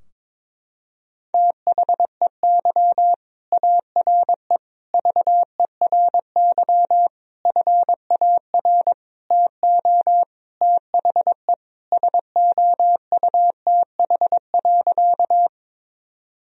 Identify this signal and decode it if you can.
Morse code, heard as a keyed tone, about 22 wpm, 700 Hz. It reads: THEY ARE VERY FAR TO THE SOUTH.
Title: Sentences from Top 1000 Words - Review Part 2 - 22wpm